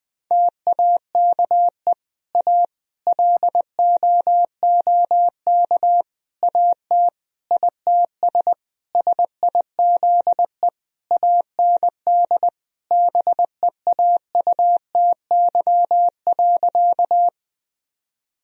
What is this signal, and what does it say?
Morse code, heard as a keyed tone, about 20 wpm, 700 Hz. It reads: TAKE A LOOK AT ITS SIZE AND BEAUTY.
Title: Sentences from Top 1000 Words - Review Part 2 - 20wpm